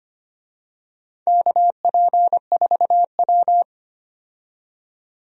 Morse code, heard as a keyed tone, about 25 wpm, 700 Hz. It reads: KP4W
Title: DXCC-100 Call Signs - 4 Characters - 25wpm